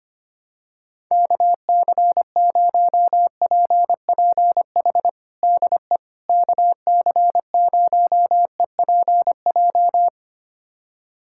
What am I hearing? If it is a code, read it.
Morse code, heard as a keyed tone, about 25 wpm, 700 Hz. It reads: KC0PPH DE KC0EPJ
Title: USA Callsign DE Callsign 25wpm